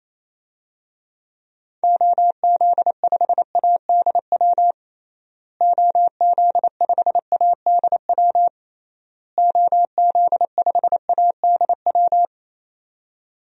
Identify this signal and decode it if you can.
Morse code, heard as a keyed tone, about 28 wpm, 700 Hz. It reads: OZ5ADW OZ5ADW OZ5ADW
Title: DXCC-100 Call Signs - 6 Characters - Repeated 3x - 28wpm